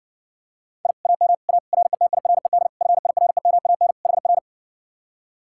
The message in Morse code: I AM A RETIRED FIREMAN HR

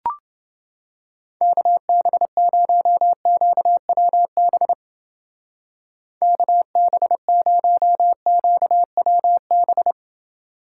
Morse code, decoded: KB0QWB KB0QWB

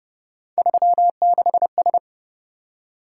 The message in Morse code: 36S

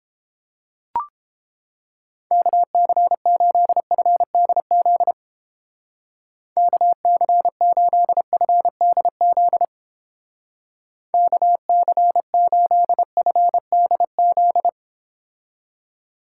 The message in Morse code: KC8FDZ KC8FDZ KC8FDZ

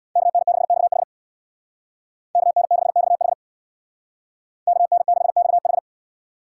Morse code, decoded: KN6XH KN6XH KN6XH

